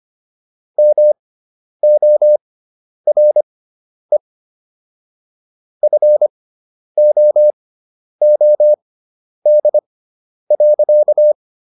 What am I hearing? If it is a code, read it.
Morse code, heard as a keyed tone, about 25 wpm, spaced as for 10 wpm, 600 Hz. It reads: MORE FOOD.